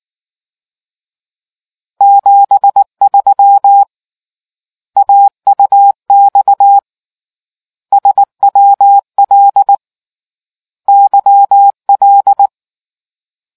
Morse code, heard as a keyed tone, about 19 wpm, 800 Hz. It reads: 73 AUX SWL YL